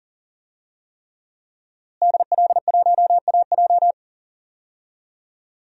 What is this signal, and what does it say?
Morse code, heard as a keyed tone, about 40 wpm, 700 Hz. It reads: DL1AJ
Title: DXCC-100 Call Signs - 5 Characters - Repeated 3x - 40wpm